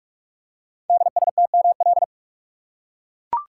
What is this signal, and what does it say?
Morse code, heard as a keyed tone, about 44 wpm, 690 Hz. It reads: DSTML
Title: Code Groups - 5 Letters - 45wpm